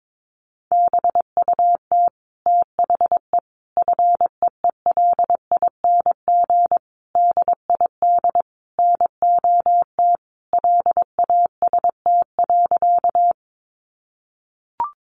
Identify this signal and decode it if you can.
Morse code, heard as a keyed tone, about 22 wpm, 700 Hz. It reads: BUT THE FEELING DID NOT LAST.
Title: Sentences from Top 1200 Words 22wpm